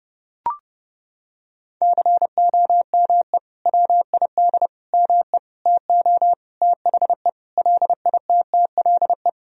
Morse code, heard as a keyed tone, about 30 wpm, 700 Hz. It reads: COME WID ME TO THE LITTLE
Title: Sentences from Top 200 Words and Abbreviations 30wpm